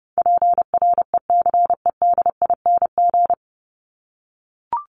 Morse code, heard as a keyed tone, about 30 wpm, 700 Hz. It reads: PRECEDING
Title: ING Suffix - Top 100 Words with Base Words 30wpm